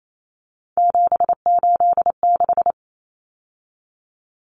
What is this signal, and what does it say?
Morse code, heard as a keyed tone, about 28 wpm, 700 Hz. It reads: 786
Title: Common Cut Numbers - 3-Digit 28wpm